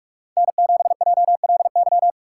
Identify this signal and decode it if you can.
Morse code, heard as a keyed tone, about 46 wpm, 700 Hz. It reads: N7JLY
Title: USA Call Signs - 5 Characters - 4x Character Spacing - 45wpm